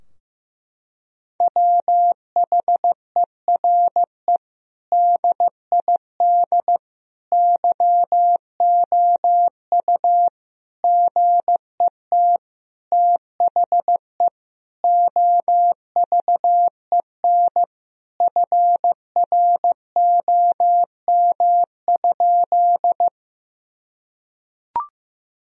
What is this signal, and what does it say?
Morse code, heard as a keyed tone, about 15 wpm, 700 Hz. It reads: WHERE DID YOU GET THE OVEN FROM?
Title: Sentences from Top 1000 Words - Review Part 4 - 15wpm